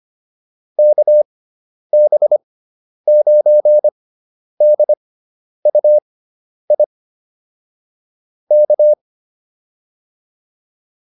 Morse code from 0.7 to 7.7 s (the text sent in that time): KB9DUI